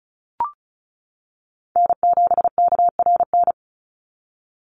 Morse code, heard as a keyed tone, about 34 wpm, 700 Hz. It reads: N7KRN
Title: NA SOTA Chaser Callsigns 35wpm